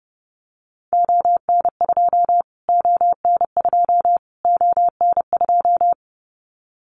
Morse code, heard as a keyed tone, about 29 wpm, 700 Hz. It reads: ON2 ON2 ON2
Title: Callsign Prefixes for Top 25 DXCC - Repeated 3x - 30wpm